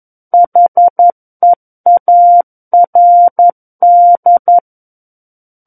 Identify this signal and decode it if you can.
Morse code, heard as a keyed tone, about 11 wpm, 700 Hz. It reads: HEARD
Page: SKCC - War of the Worlds - 02-Chapter 2 - 11 WPM